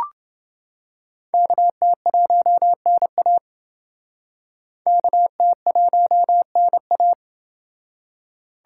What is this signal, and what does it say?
Morse code, heard as a keyed tone, about 28 wpm, 700 Hz. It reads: KT1NA KT1NA